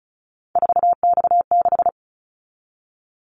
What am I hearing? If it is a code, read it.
Morse code, heard as a keyed tone, about 36 wpm, 690 Hz. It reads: 4X6
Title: Callsign Prefixes for Top 50 DXCC - 35wpm